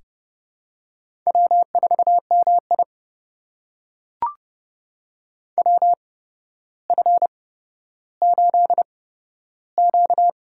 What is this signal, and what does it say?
Morse code, heard as a keyed tone, about 31 wpm, spaced as for 13 wpm, 700 Hz. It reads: W4MI WF8Q
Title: USA Call Signs - 8x Character Spacing - 30wpm